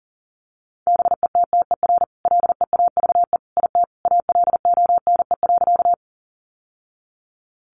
Morse code, heard as a keyed tone, about 40 wpm, 700 Hz. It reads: BETTER LEAVE IT ALONE.
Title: Sentences from Top 1000 Words 40wpm